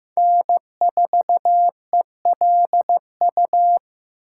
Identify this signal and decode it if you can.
Morse code, heard as a keyed tone, about 15 wpm, 700 Hz. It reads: N4ELU
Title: USA Callsign DE Callsign - 5 Characters - 15wpm